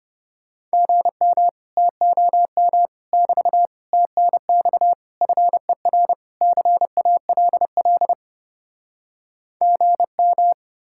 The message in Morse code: GM TOM <BT> TNX FER CALL GM

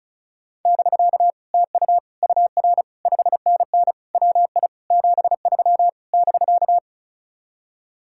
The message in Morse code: <BK> TU UR 5NN WI 73 <BK>

